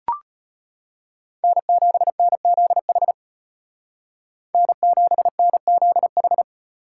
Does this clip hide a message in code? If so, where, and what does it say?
Yes: Morse code from 1.4 to 6.4 s, heard as a keyed tone, about 33 wpm, 700 Hz. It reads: N7NZH N7NZH